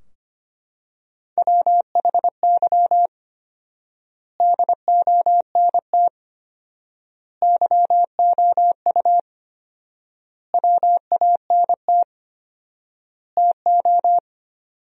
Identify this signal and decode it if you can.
Morse code, heard as a keyed tone, about 25 wpm, 700 Hz. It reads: WHY DONT YOU WANT TO
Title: 4x Word-Spacing - Sentences from Top 300 Words - 25wpm